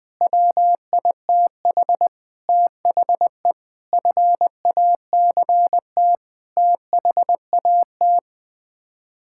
Morse code, heard as a keyed tone, about 20 wpm, 700 Hz. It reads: WITH THE FACT THAT
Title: Sets of 8 Words - 20wpm